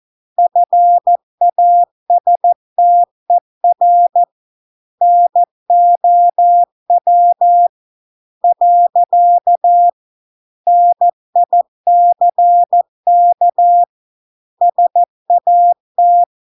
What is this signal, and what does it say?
Morse code, heard as a keyed tone, about 14 wpm, 700 Hz. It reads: FASTER NOW . NICK SAT